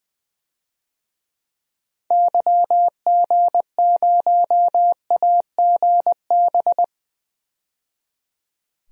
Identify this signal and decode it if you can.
Morse code, heard as a keyed tone, about 20 wpm, 700 Hz. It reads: YG0AGB
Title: DXCC-100 Call Signs - 6 Characters - 20wpm